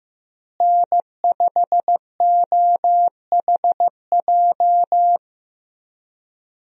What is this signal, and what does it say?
Morse code, heard as a keyed tone, about 15 wpm, 700 Hz. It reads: N5OHJ
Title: USA Call Signs - 5 Characters - Repeated 3x - 15wpm